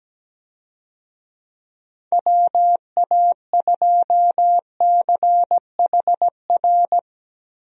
Morse code, heard as a keyed tone, about 17 wpm, 700 Hz. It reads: WA2CHR
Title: USA Call Signs - 6 Characters - Repeated 2x - 17wpm